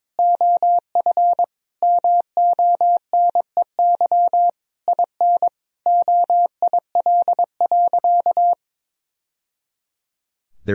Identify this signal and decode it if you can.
Morse code, heard as a keyed tone, about 22 wpm, 700 Hz. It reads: OF MONEY IN OIL.
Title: Sentences from Top 1000 Words - Review Part 4 - 22wpm